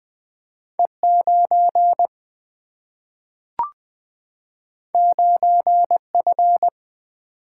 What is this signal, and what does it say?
Morse code, heard as a keyed tone, about 20 wpm, 700 Hz. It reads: E 9 9 F
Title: Single-Letter Single-Number 20wpm